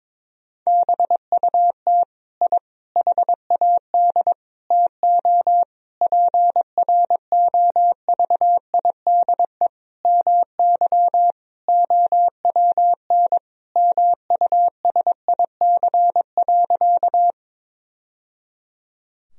Morse code, heard as a keyed tone, about 22 wpm, 700 Hz. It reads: BUT I HAD TO PROVIDE MY OWN MUSIC.
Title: Sentences from Top 1000 Words - Review Part 3 - 22wpm